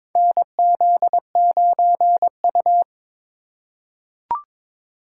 Morse code, heard as a keyed tone, about 22 wpm, 700 Hz. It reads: NZ9U
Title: USA Call Signs - 4 Characters - 4x Character Spacing - 22wpm